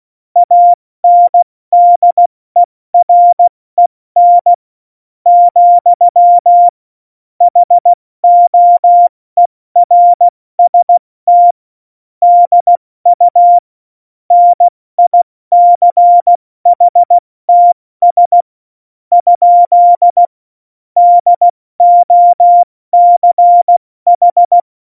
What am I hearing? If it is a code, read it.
Morse code, heard as a keyed tone, about 16 wpm, spaced as for 13 wpm, 700 Hz. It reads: ANDEREN, HOERST DU NICHTS? DOCH
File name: Morsekurs / Witze / Witz 12. Juni 2018, 70 F80